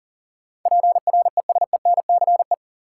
Morse code, heard as a keyed tone, about 40 wpm, 700 Hz. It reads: PRESENCE
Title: Sets of 3 Words - New Word Drill - 40wpm